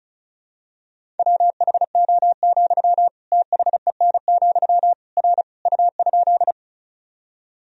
Morse code, heard as a keyed tone, about 34 wpm, 700 Hz. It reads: WHO, THEN, R U?
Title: Sentences from Top 100 Words and Abbreviations 35wpm